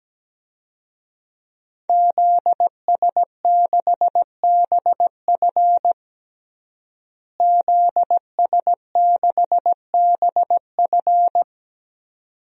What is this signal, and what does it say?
Morse code, heard as a keyed tone, about 17 wpm, 700 Hz. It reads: ZS6BF ZS6BF
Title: DXCC-100 Call Signs - 5 Characters - Repeated 2x - 17wpm